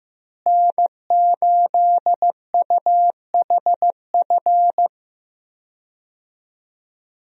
Morse code, heard as a keyed tone, about 15 wpm, 700 Hz. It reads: N8UHF
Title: USA Call Signs - 5 Characters - Repeated 2x - 15wpm